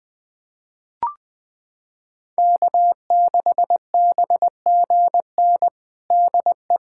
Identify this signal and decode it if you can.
Morse code, heard as a keyed tone, about 21 wpm, 700 Hz. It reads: K6BGN DE
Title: USA Callsign DE Callsign - 5 Characters - 20wpm